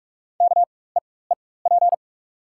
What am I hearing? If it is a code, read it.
Morse code, heard as a keyed tone, about 43 wpm, spaced as for 21 wpm, 690 Hz. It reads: KEEP
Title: 4x Character Spacing - Top 300 Words - 45wpm